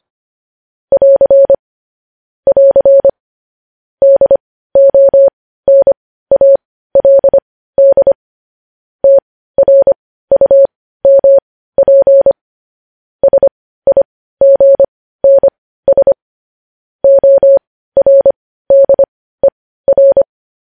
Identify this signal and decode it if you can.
Morse code, heard as a keyed tone, about 25 wpm, spaced as for 15 wpm, 550 Hz. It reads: <AR> <AR> DONALD TRUMP SIGNS ORDER
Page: news-headlines-in-morse-code-at-15-wpm